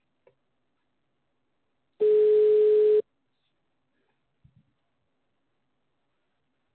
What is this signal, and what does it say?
Ringback tone (Europe, 425 Hz).